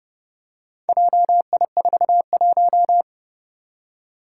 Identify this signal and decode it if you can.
Morse code, heard as a keyed tone, about 30 wpm, 700 Hz. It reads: JI41
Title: Four Challenging Letter-Number Combinations 30wpm